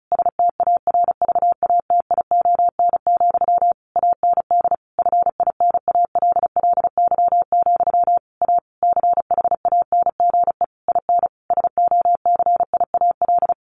Morse code, heard as a keyed tone, about 34 wpm, 700 Hz. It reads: STARVATION, AND FINALLY, A CHANGE IN SOCIAL
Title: A Sound Of Thunder 35wpm